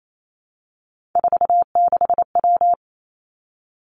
Morse code, heard as a keyed tone, about 28 wpm, 690 Hz. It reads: 46W
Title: Three Challenging Letter-Number Combinations 28wpm